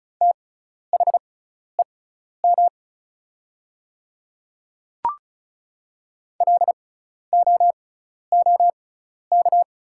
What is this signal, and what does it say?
Morse code, heard as a keyed tone, about 34 wpm, spaced as for 12 wpm, 700 Hz. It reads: THEM LOOK
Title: 6x Character Spacing - Top 200 Words - 35wpm